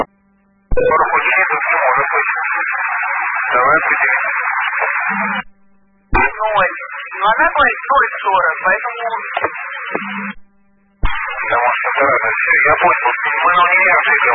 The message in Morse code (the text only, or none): none